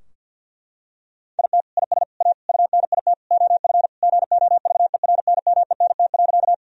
Morse code, heard as a keyed tone, about 50 wpm, 700 Hz. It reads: IT IS A UNIT OF GOVERNMENT.
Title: Sentences from Top 1000 Words - Review Part 2 - 50wpm